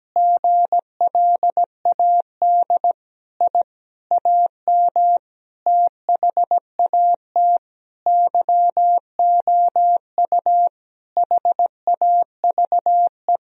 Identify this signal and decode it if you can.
Morse code, heard as a keyed tone, about 17 wpm, 700 Hz. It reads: GLAD I AM THAT YOU HAVE
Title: Sentences from Top 1000 Words - Review Part 2 - 17wpm